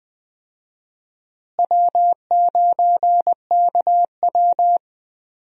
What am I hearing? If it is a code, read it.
Morse code, heard as a keyed tone, about 20 wpm, 700 Hz. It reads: W9KW
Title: USA Call Signs - 4 Characters - Repeated 3x - 20wpm